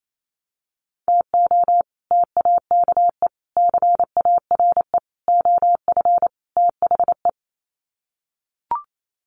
Morse code, heard as a keyed tone, about 28 wpm, 700 Hz. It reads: TO TAKE CARE OF THE
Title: Sets of 5 Words Encore 28wpm